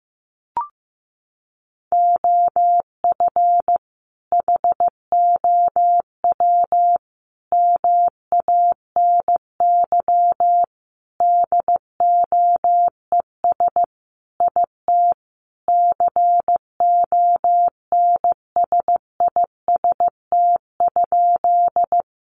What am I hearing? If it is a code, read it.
Morse code, heard as a keyed tone, about 15 wpm, 700 Hz. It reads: OF HOW MANY DOES IT CONSIST?
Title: Sentences from Top 1400 Words 15wpm